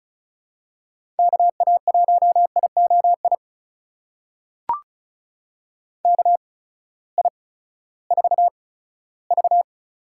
Morse code, heard as a keyed tone, about 36 wpm, spaced as for 18 wpm, 700 Hz. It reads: KA1IOI KI4V
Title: USA Call Signs - 6 Characters - 8x Character Spacing - 35wpm